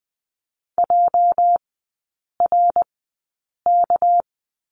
Morse code, heard as a keyed tone, about 20 wpm, spaced as for 8 wpm, 700 Hz. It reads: JRK
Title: 205 Lesson 33e - Letter j - Three Characters 20wpm